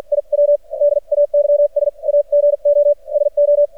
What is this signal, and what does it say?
Morse code, heard as a keyed tone, about 50 wpm, spaced as for 41 wpm, 580 Hz. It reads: IW DAYS AGO SO
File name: archived / CRSnet